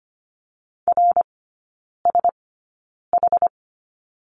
Morse code, heard as a keyed tone, about 25 wpm, spaced as for 9 wpm, 690 Hz. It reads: RSH
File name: Lesson 14e - Number 2 - Three Characters 25wpm